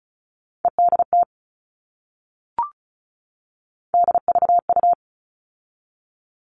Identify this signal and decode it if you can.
Morse code, heard as a keyed tone, about 34 wpm, 700 Hz. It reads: EDT DVU
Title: All Cut Numbers - 3-Digit 35wpm